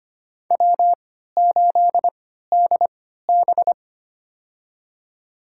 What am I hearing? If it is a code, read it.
Morse code, heard as a keyed tone, about 25 wpm, spaced as for 14 wpm, 700 Hz. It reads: W8DB
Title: USA Call Signs - 4 Characters - 3x Character Spacing - 25wpm